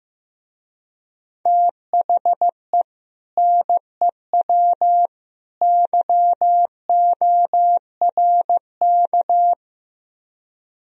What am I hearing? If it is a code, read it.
Morse code, heard as a keyed tone, about 15 wpm, 700 Hz. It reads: THE NEW YORK
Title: Speed-Racing - Sets of 3 Words 15wpm